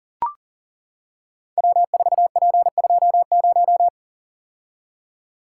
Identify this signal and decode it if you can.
Morse code, heard as a keyed tone, about 41 wpm, 700 Hz. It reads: W4P20
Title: Code Groups - 5 Letter:Numbers - 40wpm